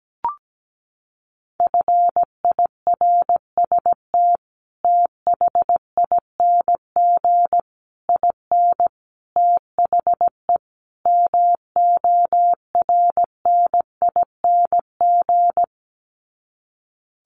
Morse code, heard as a keyed tone, about 17 wpm, 700 Hz. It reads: FIRST THING IN THE MORNING
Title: Sets of 5 Words Encore2 17wpm